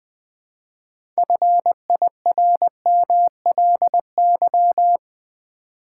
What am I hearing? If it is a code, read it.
Morse code, heard as a keyed tone, about 20 wpm, 700 Hz. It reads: FIRMLY
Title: Build A Word - 3 Letters At A Time - 6-Letter Words - 20wpm